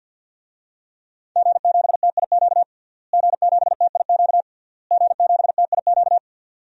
Morse code, heard as a keyed tone, about 52 wpm, 700 Hz. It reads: G7TIQ G7TIQ G7TIQ
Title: DXCC-100 Call Signs - 5 Characters - Repeated 3x - 50wpm